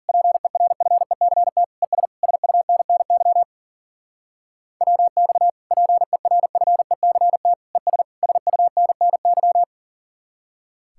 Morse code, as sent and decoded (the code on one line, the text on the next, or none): .--. .-. ..-. -.-. - / . ... / ... ..- -. -. -.-- / .-- -..- / .--. .-. ..-. -.-. - / . ... / ... ..- -. -. -.--
PERFECT ES SUNNY WX PERFECT ES SUNNY